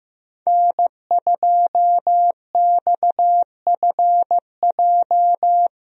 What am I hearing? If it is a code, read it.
Morse code, heard as a keyed tone, about 15 wpm, 700 Hz. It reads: N2XFJ
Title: USA Call Signs - 5 Characters - Repeated 3x - 15wpm